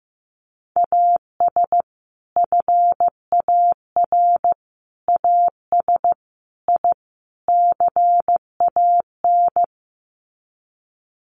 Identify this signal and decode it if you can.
Morse code, heard as a keyed tone, about 15 wpm, 700 Hz. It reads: AS FAR AS I CAN
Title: Sets of 5 Words Encore 15wpm